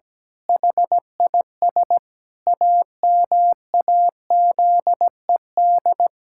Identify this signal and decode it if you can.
Morse code, heard as a keyed tone, about 17 wpm, 700 Hz. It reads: HIS AMAZED
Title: Pangram Challenge v2 - 17wpm